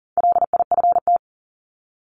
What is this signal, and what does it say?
Morse code, heard as a keyed tone, about 40 wpm, 700 Hz. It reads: LIFT
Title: Top 751-1000 Words 40wpm